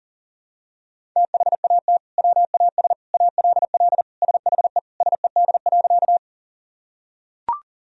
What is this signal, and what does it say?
Morse code, heard as a keyed tone, about 40 wpm, 700 Hz. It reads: THAT WAS ALL SHE SED.
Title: Sentences from Top 100 Words and Abbreviations 40wpm